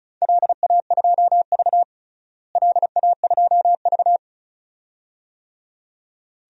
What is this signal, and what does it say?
Morse code, heard as a keyed tone, about 36 wpm, 700 Hz. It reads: LA2V LA2V
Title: DXCC-100 Call Signs - 4 Characters - Repeated 2x - 35wpm